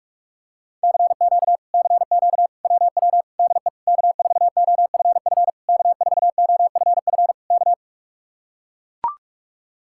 Morse code, heard as a keyed tone, about 44 wpm, 700 Hz. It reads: CQ CQ WW DE K4OFF K4OFF K